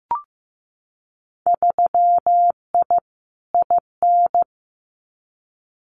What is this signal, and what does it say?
Morse code, heard as a keyed tone, about 15 wpm, 700 Hz. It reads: 3I IN
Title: Winter Field Day Exchanges 15wpm